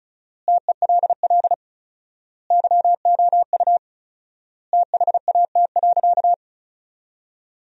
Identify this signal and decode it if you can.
Morse code, heard as a keyed tone, about 36 wpm, 700 Hz. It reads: TELL YOU THAT.
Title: 4x Word-Spacing - Sentences from Top 300 Words - 35wpm